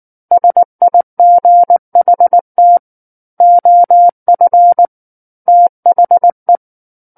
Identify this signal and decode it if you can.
Morse code, heard as a keyed tone, about 19 wpm, 700 Hz. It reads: SIGHT OF THE
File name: Book One / 10-Chapter 10 - 19 WPM